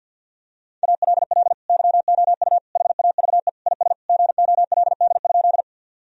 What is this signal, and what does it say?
Morse code, heard as a keyed tone, about 52 wpm, 700 Hz. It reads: ALL YOU HAVE IS GOLD?
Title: Sentences from Top 1000 Words - Review Part 3 - 50wpm